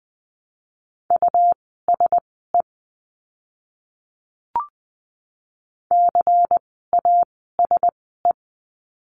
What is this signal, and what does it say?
Morse code, heard as a keyed tone, about 20 wpm, spaced as for 15 wpm, 700 Hz. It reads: USE CASE
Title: 101 Lesson 17f - Letter u - Words 20wpm